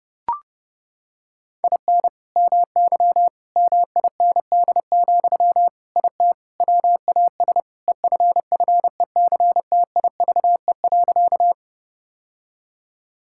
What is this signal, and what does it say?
Morse code, heard as a keyed tone, about 30 wpm, 700 Hz. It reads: IN MY MIND, IT WAS EFFECTIVE.
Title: Sentences from Top 1000 Words - Review Part 6 - 30wpm